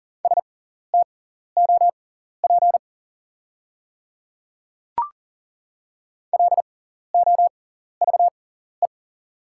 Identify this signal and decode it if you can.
Morse code, heard as a keyed tone, about 40 wpm, spaced as for 14 wpm, 690 Hz. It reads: STOP LOVE
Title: 6x Character Spacing - Top 500 Words - 40wpm